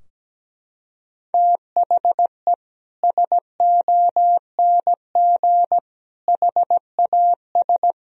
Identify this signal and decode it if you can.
Morse code, heard as a keyed tone, about 17 wpm, 700 Hz. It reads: THE SONG HAS
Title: Sentences from Top 1000 Words - Review Part 3 - 17wpm